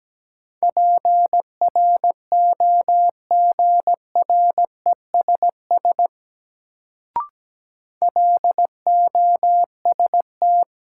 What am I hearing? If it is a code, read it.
Morse code, heard as a keyed tone, about 17 wpm, 700 Hz. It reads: PROGRESS LOST
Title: Sentences from Top 1400 Words - New Word Drill - 17wpm